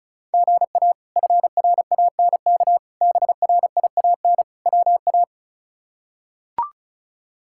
Morse code, heard as a keyed tone, about 36 wpm, 700 Hz. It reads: GA FRANK BRIAN WA
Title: K1USN SST QSO Practice - 35wpm